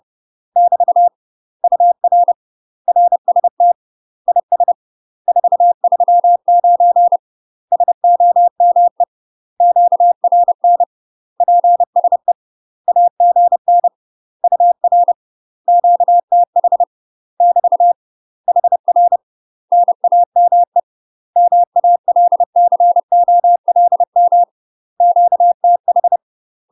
Morse code, heard as a keyed tone, about 30 wpm, 700 Hz. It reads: <BT> UR RST IS 439 SOME QRN PSE AGN UR QTH <BT> HR NAME MALCOLM QTH
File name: UK MASTER QSOs TRAINING / 30WPM / UK MASTER QSOs 0038